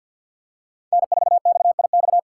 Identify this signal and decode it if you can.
Morse code, heard as a keyed tone, about 48 wpm, 700 Hz. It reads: N4XIX
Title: USA Call Signs - 5 Characters - 3x Character Spacing - 50wpm